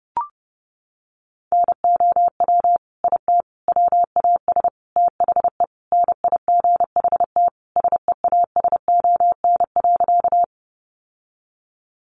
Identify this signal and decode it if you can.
Morse code, heard as a keyed tone, about 30 wpm, 700 Hz. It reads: NOW IT WAS THE NIGHT SEASON.
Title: Sentences from Top 1400 Words 30wpm